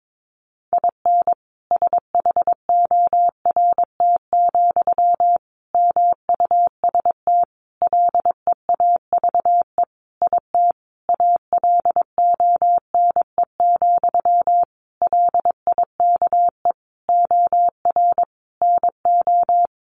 Morse code, heard as a keyed tone, about 22 wpm, 700 Hz. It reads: IN SHORT, MUST LEAVE IT ALONE, LIKE OR NO
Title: Sentences from Top 1000 Words 22wpm